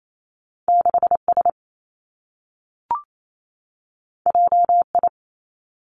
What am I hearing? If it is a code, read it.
Morse code, heard as a keyed tone, about 28 wpm, 700 Hz. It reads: 6 S J I
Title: Two Challenging Letter-Number Combinations 28wpm